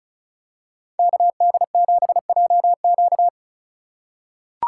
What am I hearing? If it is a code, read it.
Morse code, heard as a keyed tone, about 36 wpm, 700 Hz. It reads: KD7JQ
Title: USA Call Signs - 5 Characters - 4x Character Spacing - 35wpm